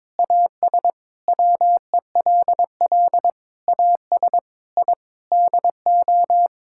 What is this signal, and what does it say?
Morse code, heard as a keyed tone, about 22 wpm, 700 Hz. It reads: AS WELL AS I DO
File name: practice / 22wpm / Sets of 8 Words - 22wpm